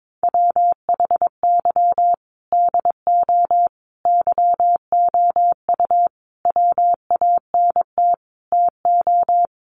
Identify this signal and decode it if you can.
Morse code, heard as a keyed tone, about 22 wpm, 700 Hz. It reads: WHY DO YOU WANT TO
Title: Sentences from Top 200 Words 22wpm